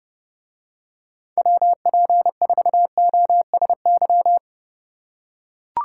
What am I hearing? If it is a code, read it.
Morse code, heard as a keyed tone, about 28 wpm, 700 Hz. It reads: WP4OSY